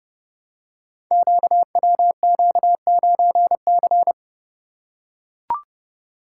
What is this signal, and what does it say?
Morse code, heard as a keyed tone, about 30 wpm, 700 Hz. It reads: QWQ9C
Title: Code Groups - 5 Letter:Numbers - 30wpm